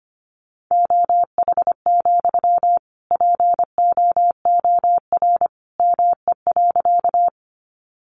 Morse code, heard as a keyed tone, about 25 wpm, 700 Hz. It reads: OH, POOR ME.
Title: Sentences from Top 1000 Words 25wpm